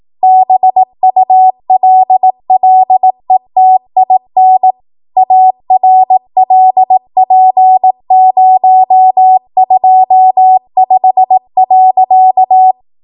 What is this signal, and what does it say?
Morse code, heard as a keyed tone, about 18 wpm, 750 Hz. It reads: BULLETIN ARLP025.
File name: Morse / 18WPM